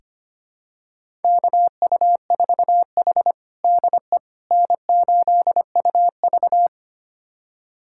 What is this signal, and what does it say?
Morse code, heard as a keyed tone, about 25 wpm, 700 Hz. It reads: KU4H DE N8UV
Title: USA Callsign DE Callsign - 4 Characters - 25wpm